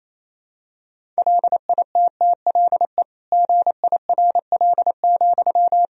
Morse code, heard as a keyed tone, about 28 wpm, 700 Hz. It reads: LITTLE GIRL,
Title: Sentences from Top 400 Words and Abbreviations 28wpm